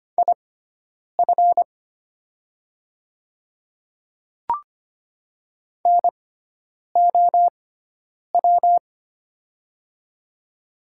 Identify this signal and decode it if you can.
Morse code, heard as a keyed tone, about 25 wpm, spaced as for 9 wpm, 700 Hz. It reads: IF NOW
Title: 6x Character Spacing - Top 100 Words - 25wpm